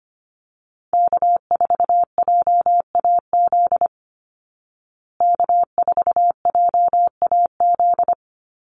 Morse code, heard as a keyed tone, about 25 wpm, 700 Hz. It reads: K4JAZ K4JAZ
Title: Call Signs Repeated 3x - 25wpm